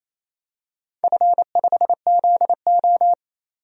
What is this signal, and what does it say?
Morse code, heard as a keyed tone, about 28 wpm, 700 Hz. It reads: F5ZO
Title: DXCC-100 Call Signs - 4 Characters - 28wpm